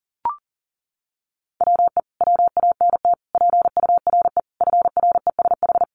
Morse code, heard as a keyed tone, about 40 wpm, 700 Hz. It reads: WE WANT PURE FRESH
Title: Sentences from Top 1500 Words 40wpm